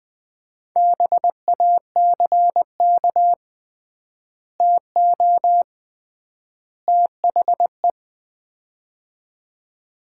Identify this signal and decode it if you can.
Morse code, heard as a keyed tone, about 20 wpm, 700 Hz. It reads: BACK TO THE
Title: 3x Word-Spacing - Sets of 3 Words - 20wpm